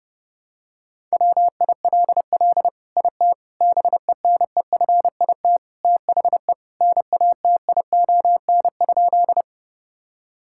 Morse code, heard as a keyed tone, about 30 wpm, 700 Hz. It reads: WILL IT BENEFIT THE NATION?
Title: Sentences from Top 1000 Words - Review Part 6 - 30wpm